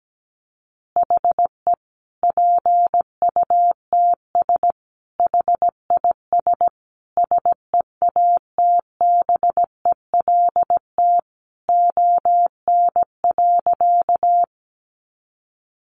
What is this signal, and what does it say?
Morse code, heard as a keyed tone, about 17 wpm, 700 Hz. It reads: HE PUTS HIS SEATBELT ON.
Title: Additional Easy Phrases 17wpm